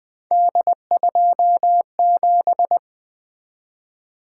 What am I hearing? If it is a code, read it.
Morse code, heard as a keyed tone, about 20 wpm, 700 Hz. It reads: D27
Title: Three Letter-Number Combinations - 20wpm